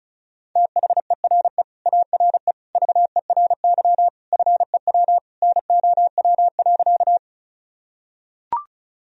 Morse code, heard as a keyed tone, about 34 wpm, 700 Hz. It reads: THERE ARE VERY FEW NOW.
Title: Sentences from Top 1000 Words - Review Part 1 - 35wpm